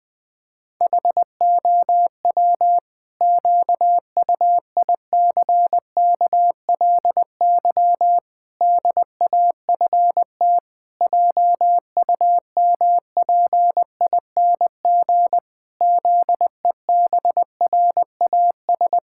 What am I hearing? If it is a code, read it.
Morse code, heard as a keyed tone, about 21 wpm, 700 Hz. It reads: HOW QUICKLY DAFT JUMPING ZEBRAS
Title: Pangram Challenge v2 - 20wpm